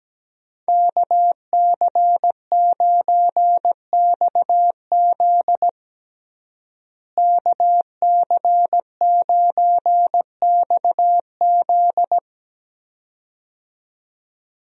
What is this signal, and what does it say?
Morse code, heard as a keyed tone, about 17 wpm, 700 Hz. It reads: KC9XZ KC9XZ
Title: USA Call Signs - 5 Characters - Repeated 2x - 17wpm